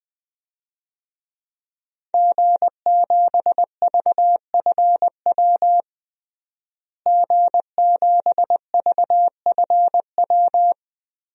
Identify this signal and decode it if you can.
Morse code, heard as a keyed tone, about 20 wpm, 700 Hz. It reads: G7VFW G7VFW
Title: DXCC-100 Call Signs - 5 Characters - Repeated 3x - 20wpm